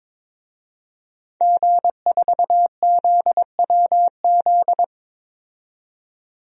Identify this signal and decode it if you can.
Morse code, heard as a keyed tone, about 21 wpm, 700 Hz. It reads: G4ZWZ
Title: DXCC-100 Call Signs - 5 Characters - 22wpm